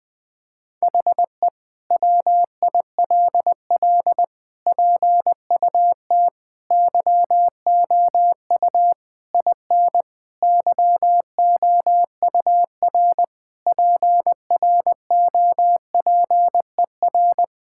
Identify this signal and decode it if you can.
Morse code, heard as a keyed tone, about 21 wpm, 700 Hz. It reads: HE WILL PUT YOU IN YOUR PROPER
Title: Sentences from Top 1000 Words - Review Part 5 - 20wpm